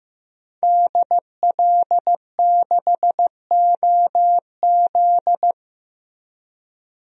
Morse code, heard as a keyed tone, about 15 wpm, 700 Hz. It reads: DL6OZ
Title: DXCC-100 Call Signs - 5 Characters - Repeated 2x - 15wpm